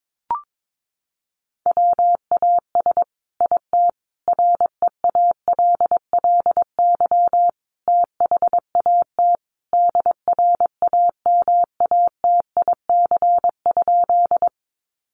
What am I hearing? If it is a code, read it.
Morse code, heard as a keyed tone, about 22 wpm, 700 Hz. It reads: WAS IT REALLY THAT DRAMATIC?
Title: Sentences from Top 1300 Words 22wpm